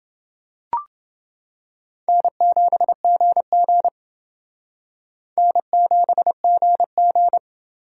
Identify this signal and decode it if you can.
Morse code, heard as a keyed tone, about 28 wpm, 700 Hz. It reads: N7GG N7GG